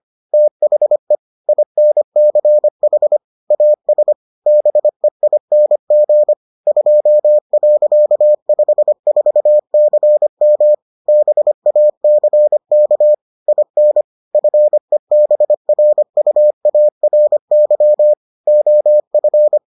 Morse code, heard as a keyed tone, about 25 wpm, 600 Hz. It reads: THE INCH AS BEING 2.54CM BACK IN FEBRUARY OF